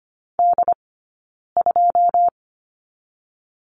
Morse code, heard as a keyed tone, about 25 wpm, spaced as for 9 wpm, 700 Hz. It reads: D2
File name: Lesson 15d - Number 5 - Two Characters 25wpm